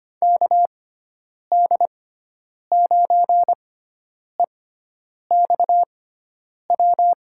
Morse code, heard as a keyed tone, about 25 wpm, spaced as for 9 wpm, 700 Hz. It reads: KD9EXW
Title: USA Call Signs - 6 Characters - 6x Character Spacing - 25wpm